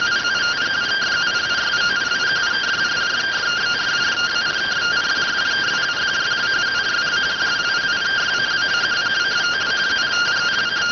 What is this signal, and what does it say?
Two-tone teleprinter signal, about 170 Hz shift.